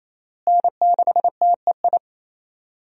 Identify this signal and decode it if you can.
Morse code, heard as a keyed tone, about 28 wpm, 700 Hz. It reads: N6TEI
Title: USA Call Signs - 5 Characters - Repeated 2x - 28wpm